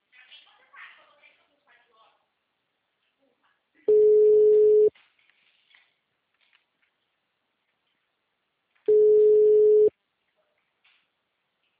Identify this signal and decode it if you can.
Ringback tone (Europe, 425 Hz), with speech.